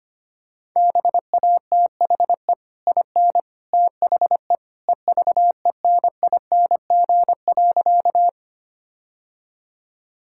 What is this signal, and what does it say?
Morse code, heard as a keyed tone, about 25 wpm, 690 Hz. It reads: BATHE IN THE EVENING.
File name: Easy Sentences with Hints - 25wpm